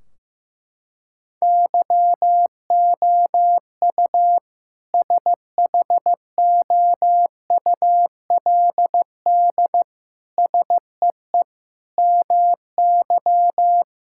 Morse code, heard as a keyed tone, about 15 wpm, 700 Hz. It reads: YOU SHOULD SEE MY
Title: Sentences from Top 1600 Words 15wpm